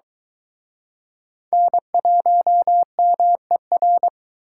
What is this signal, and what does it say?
Morse code, heard as a keyed tone, about 23 wpm, 700 Hz. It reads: N1MER